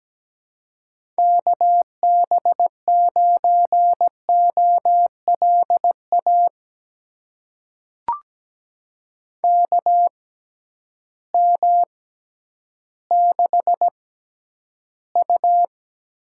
Morse code, heard as a keyed tone, about 17 wpm, spaced as for 10 wpm, 700 Hz. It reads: KB9OLA KM6U
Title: USA Call Signs - 6 Characters - 6x Character Spacing - 17wpm